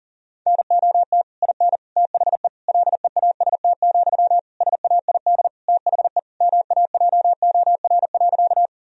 Morse code, heard as a keyed tone, about 40 wpm, 700 Hz. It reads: NOT IN THE LEAST, SAID THE MAJOR.
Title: Sentences from Top 1000 Words - Review Part 4 - 40wpm